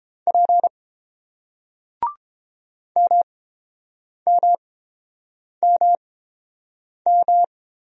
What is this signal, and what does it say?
Morse code, heard as a keyed tone, about 31 wpm, spaced as for 7 wpm, 700 Hz. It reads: P MMMM